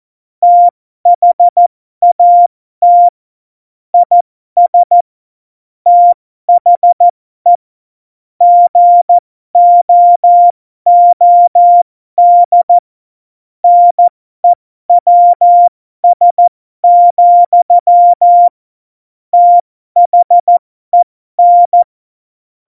Morse code, heard as a keyed tone, about 14 wpm, spaced as for 12 wpm, 700 Hz. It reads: THAT IS THE GOOD NEWS, THEN
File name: Morsekurs / Witze